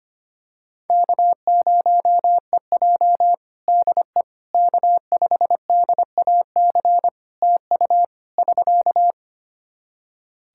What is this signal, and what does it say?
Morse code, heard as a keyed tone, about 25 wpm, 700 Hz. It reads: K0EJ DE K5DAC TU <SK>
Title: Practice QSO - Exchange 4 - Ending - 25wpm